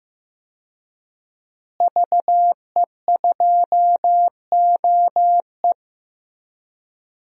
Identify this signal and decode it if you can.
Morse code, heard as a keyed tone, about 15 wpm, 700 Hz. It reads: VE2OE
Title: DXCC-100 Call Signs - 5 Characters - Repeated 2x - 15wpm